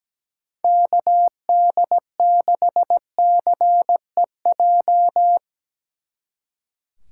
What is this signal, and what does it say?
Morse code, heard as a keyed tone, about 18 wpm, 700 Hz. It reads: KD6CEJ
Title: Speed Racing - USA 2x3 Call Signs 17wpm